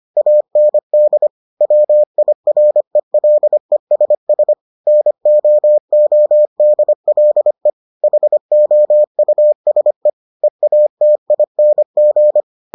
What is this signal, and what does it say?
Morse code, heard as a keyed tone, about 25 wpm, 600 Hz. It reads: AND WIRELESS NOODLE HOUSE EATING